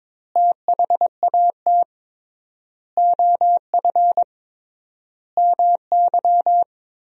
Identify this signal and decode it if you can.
Morse code, heard as a keyed tone, about 22 wpm, 700 Hz. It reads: THAT OF MY
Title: 3x Word-Spacing - Sentences from Top 500 Words - 22wpm